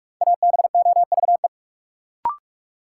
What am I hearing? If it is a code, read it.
Morse code, heard as a keyed tone, about 48 wpm, 700 Hz. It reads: ABOVE
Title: Sentences from Top 1200 Words - New Word Drill Alt - 45wpm